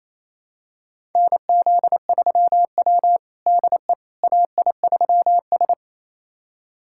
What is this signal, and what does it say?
Morse code, heard as a keyed tone, about 27 wpm, 690 Hz. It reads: NZ3W DE AI3S
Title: USA Callsign DE Callsign - 4 Characters - 28wpm